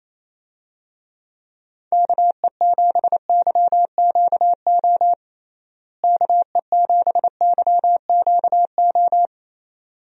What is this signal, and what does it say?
Morse code, heard as a keyed tone, about 27 wpm, 700 Hz. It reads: KE7YQO KE7YQO
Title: USA Call Signs - 6 Characters - Repeated 2x - 28wpm